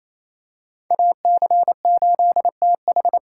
Morse code, heard as a keyed tone, about 28 wpm, 700 Hz. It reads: AC8TH
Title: USA Call Signs - 5 Characters - 2x Character Spacing - 28wpm